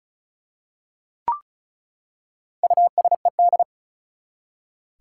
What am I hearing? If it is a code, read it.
Morse code, heard as a keyed tone, about 34 wpm, 700 Hz. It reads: USED
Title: Four Letter Words - 35wpm